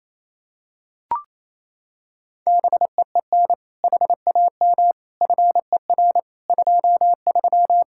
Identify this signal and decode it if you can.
Morse code, heard as a keyed tone, about 28 wpm, 700 Hz. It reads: BEEN HAM FER 23
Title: Practice QSO - Exchange 3 - More Info - 28wpm